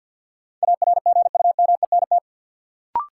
Morse code, heard as a keyed tone, about 47 wpm, 700 Hz. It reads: ARGUMENT
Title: Sentences from Top 1400 Words - New Word Drill Alt - 50wpm